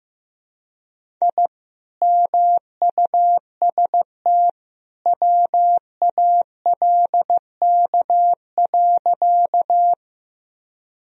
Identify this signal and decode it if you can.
Morse code, heard as a keyed tone, about 15 wpm, 700 Hz. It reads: I MUST WALK.
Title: Sentences from Top 1000 Words - Review Part 1 - 15wpm